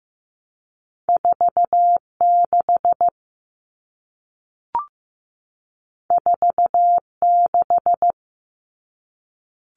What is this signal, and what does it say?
Morse code, heard as a keyed tone, about 15 wpm, 700 Hz. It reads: 46 46
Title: All Cut Numbers - 2-Digit 15wpm